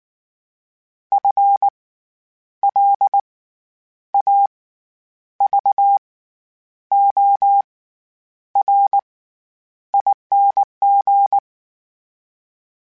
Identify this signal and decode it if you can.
Morse code, heard as a keyed tone, about 19 wpm, spaced as for 9 wpm, 800 Hz. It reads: FLAVORING
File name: Special INGs / Special INGs-mp3s / SlowToFaster ing @18